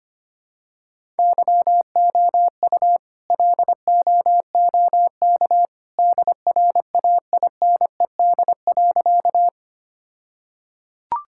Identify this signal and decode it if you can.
Morse code, heard as a keyed tone, about 25 wpm, 700 Hz. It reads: YOU LOOK DRAINED.
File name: practice / 25wpm / All-Age Appropriate Jokes - 25wpm